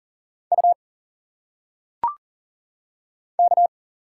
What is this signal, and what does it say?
Morse code, heard as a keyed tone, about 40 wpm, 700 Hz. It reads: U K
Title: Single Letters - Phonetic Pronunciation 40wpm